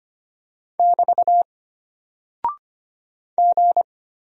Morse code, heard as a keyed tone, about 25 wpm, 700 Hz. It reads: <BT> G
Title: Single Character (Letter:Number:Punctuation:Prosign) Alt 25wpm